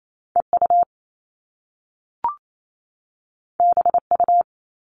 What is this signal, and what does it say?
Morse code, heard as a keyed tone, about 28 wpm, 700 Hz. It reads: EU BU
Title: All Cut Numbers - 2-Digit 28wpm